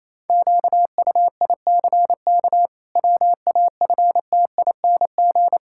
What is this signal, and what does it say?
Morse code, heard as a keyed tone, about 28 wpm, 700 Hz. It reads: QUICK WAFTING
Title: Pangram Challenge v2 - 28wpm